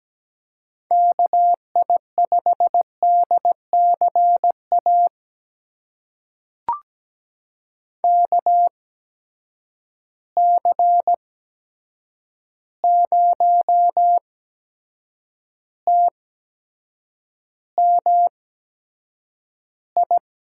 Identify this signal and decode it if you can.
Morse code, heard as a keyed tone, about 17 wpm, spaced as for 7 wpm, 700 Hz. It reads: KI5DCA KC0TMI
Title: USA Call Signs - 8x Character Spacing - 17wpm